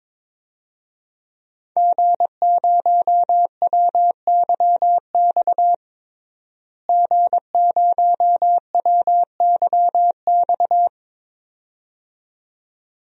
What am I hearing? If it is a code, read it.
Morse code, heard as a keyed tone, about 22 wpm, 700 Hz. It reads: G0WYX G0WYX
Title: DXCC-100 Call Signs - 5 Characters - Repeated 2x - 22wpm